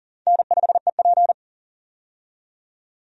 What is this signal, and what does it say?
Morse code, heard as a keyed tone, about 40 wpm, 700 Hz. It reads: N5EP
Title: USA Call Signs - 4 Characters - 40wpm